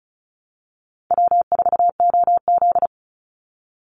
Morse code, heard as a keyed tone, about 34 wpm, 700 Hz. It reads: W4OZ
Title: USA Call Signs 35wpm